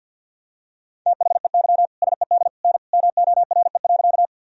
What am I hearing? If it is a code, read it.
Morse code, heard as a keyed tone, about 51 wpm, 700 Hz. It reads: THEY SED N MORE.
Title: Sentences from Top 100 Words and Abbreviations 50wpm